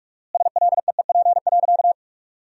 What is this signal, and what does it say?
Morse code, heard as a keyed tone, about 45 wpm, 700 Hz. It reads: SLEEP.
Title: 4x Word-Spacing - Sentences from Top 500 Words - 45wpm